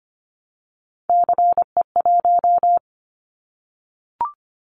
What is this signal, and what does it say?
Morse code, heard as a keyed tone, about 24 wpm, 700 Hz. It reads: CE1
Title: Callsign Prefixes for Top 100 DXCC - 25wpm